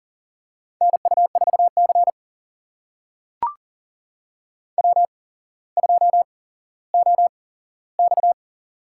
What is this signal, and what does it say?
Morse code, heard as a keyed tone, about 42 wpm, spaced as for 17 wpm, 700 Hz. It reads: NU4C W2OX
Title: USA Call Signs - 4 Characters - 8x Character Spacing - 40wpm